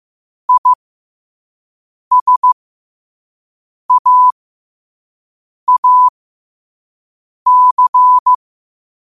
Morse code, heard as a keyed tone, about 15 wpm, spaced as for 5 wpm, 1000 Hz. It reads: ISAAC
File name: qotdfiles / 5wpm